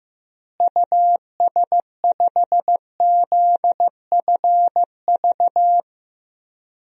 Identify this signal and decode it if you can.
Morse code, heard as a keyed tone, about 15 wpm, 700 Hz. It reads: US5ZFV
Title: DXCC-100 Call Signs - 6 Characters - Repeated 3x - 15wpm